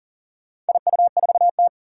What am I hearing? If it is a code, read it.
Morse code, heard as a keyed tone, about 38 wpm, 700 Hz. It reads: IU4T
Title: Four Challenging Letter-Number Combinations 40wpm